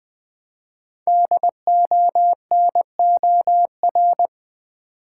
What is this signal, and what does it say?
Morse code, heard as a keyed tone, about 21 wpm, 700 Hz. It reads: DONOR
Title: 5 Letter Words - First 3 Letters Given - 20wpm